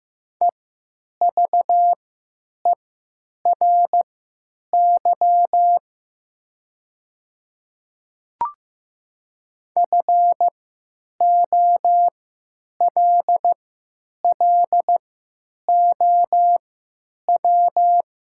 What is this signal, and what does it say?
Morse code, heard as a keyed tone, about 15 wpm, spaced as for 9 wpm, 700 Hz. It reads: EVERY FOLLOW
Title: 3x Character Spacing - Top 200 Words - 15wpm